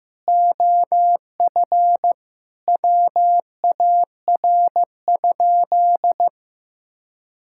Morse code, heard as a keyed tone, about 15 wpm, 700 Hz. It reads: OF WAR?
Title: Sentences from Top 1000 Words - Review Part 5 - 15wpm